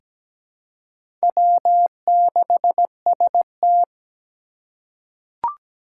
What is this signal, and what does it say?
Morse code, heard as a keyed tone, about 17 wpm, 700 Hz. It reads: W6ST
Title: USA Call Signs - 4 Characters - 2x Character Spacing - 17wpm